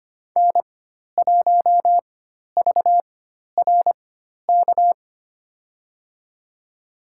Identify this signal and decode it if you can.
Morse code, heard as a keyed tone, about 25 wpm, spaced as for 12 wpm, 700 Hz. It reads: N1VRK
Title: USA Call Signs - 5 Characters - 4x Character Spacing - 25wpm